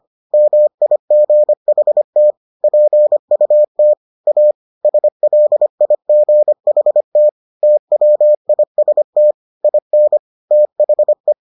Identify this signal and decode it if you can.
Morse code, heard as a keyed tone, about 25 wpm, 600 Hz. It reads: MIGHT PUT A SLIGHT TWIST IN THE